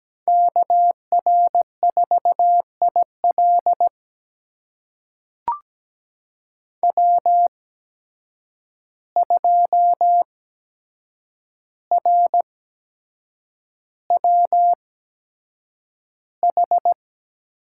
Morse code, heard as a keyed tone, about 17 wpm, spaced as for 7 wpm, 700 Hz. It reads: KR4IL W2RWH
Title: USA Call Signs - 5 Characters - 8x Character Spacing - 17wpm